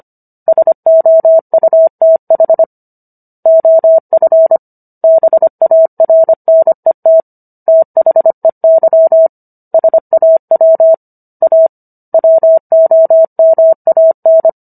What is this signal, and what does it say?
Morse code, heard as a keyed tone, about 25 wpm, 650 Hz. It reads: SOUTH OF BARNET THEY SAW A WOMAN